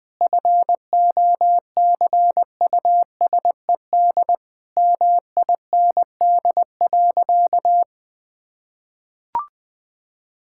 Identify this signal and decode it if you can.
Morse code, heard as a keyed tone, about 20 wpm, 700 Hz. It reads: FOCUSED MIND.
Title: Difficult Sentences with Hints - 20wpm